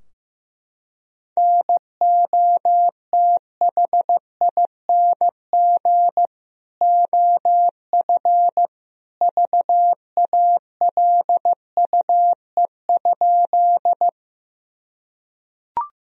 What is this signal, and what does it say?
Morse code, heard as a keyed tone, about 15 wpm, 700 Hz. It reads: NOTHING OF VALUE?
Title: Sentences from Top 1000 Words - Review Part 3 - 15wpm